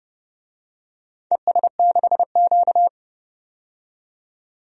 Morse code, heard as a keyed tone, about 30 wpm, 700 Hz. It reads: ES6Q
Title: DXCC-100 Call Signs - 4 Characters - 30wpm